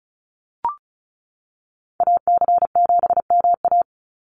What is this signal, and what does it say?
Morse code, heard as a keyed tone, about 34 wpm, 700 Hz. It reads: AC7MA
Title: NA SOTA Activator Callsigns 35wpm